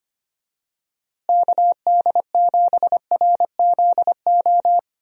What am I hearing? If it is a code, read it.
Morse code, heard as a keyed tone, about 25 wpm, 700 Hz. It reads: KD7RZO
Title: USA Call Signs - 6 Characters - Repeated 3x - 25wpm